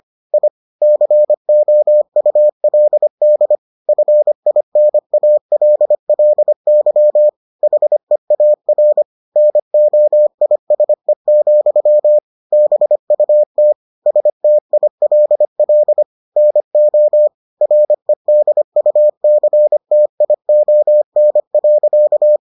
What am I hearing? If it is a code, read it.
Morse code, heard as a keyed tone, about 25 wpm, 600 Hz. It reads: I COULD FINALLY HEAR NOISE, BUT STILL NO REDUCTION.